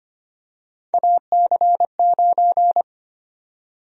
Morse code, heard as a keyed tone, about 25 wpm, 700 Hz. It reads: AC9
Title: USA Call Sign Prefixes - Repeated 2x - 25wpm